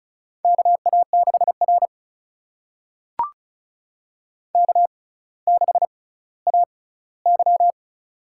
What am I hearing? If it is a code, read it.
Morse code, heard as a keyed tone, about 36 wpm, spaced as for 18 wpm, 700 Hz. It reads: KA6R K6AY
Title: USA Call Signs - 4 Characters - 6x Character Spacing - 35wpm